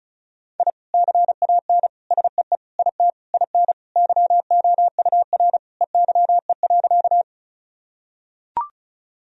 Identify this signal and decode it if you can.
Morse code, heard as a keyed tone, about 34 wpm, 700 Hz. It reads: I CAN SEE IT IN YOUR EYE.
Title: Sentences from Top 1000 Words - Review Part 1 - 35wpm